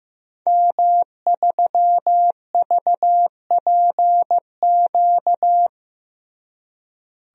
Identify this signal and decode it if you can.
Morse code, heard as a keyed tone, about 15 wpm, 700 Hz. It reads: M3VPQ
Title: DXCC-100 Call Signs - 5 Characters - Repeated 3x - 15wpm